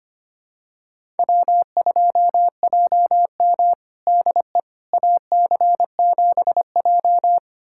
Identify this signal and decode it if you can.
Morse code, heard as a keyed tone, about 25 wpm, 700 Hz. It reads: W2JM DE AC7J
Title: USA Callsign DE Callsign 25wpm